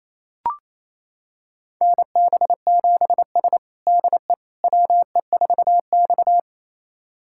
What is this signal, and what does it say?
Morse code, heard as a keyed tone, about 28 wpm, 700 Hz. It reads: NB7S DE WE4X
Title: USA Callsign DE Callsign - 4 Characters - 28wpm